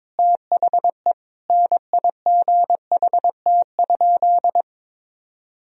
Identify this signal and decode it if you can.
Morse code, heard as a keyed tone, about 21 wpm, 700 Hz. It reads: THE NIGHT?
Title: Sentences from Top 1000 Words - Review Part 2 - 22wpm